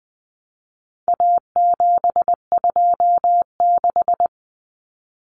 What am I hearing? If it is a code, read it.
Morse code, heard as a keyed tone, about 20 wpm, 700 Hz. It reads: A726
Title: Common Cut Numbers - 4-Digit 20wpm